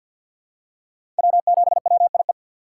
Morse code, heard as a keyed tone, about 52 wpm, 700 Hz. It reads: W7WIE
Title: USA Call Signs - 5 Characters - 2x Character Spacing - 50wpm